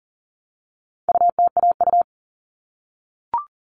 Morse code, heard as a keyed tone, about 40 wpm, 700 Hz. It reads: UTAU
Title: All Cut Numbers - 4-Digit 40wpm